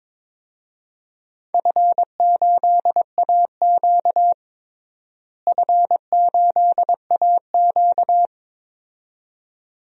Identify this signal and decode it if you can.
Morse code, heard as a keyed tone, about 22 wpm, 700 Hz. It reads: F8AQ F8AQ
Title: DXCC-100 Call Signs - 4 Characters - Repeated 2x - 22wpm